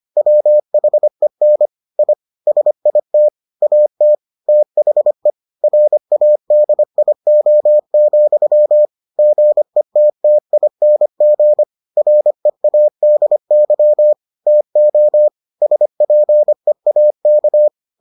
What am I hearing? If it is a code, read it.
Morse code, heard as a keyed tone, about 25 wpm, 600 Hz. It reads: WHEN I SIT AT THE RADIO, GETTING READY TO SPEAK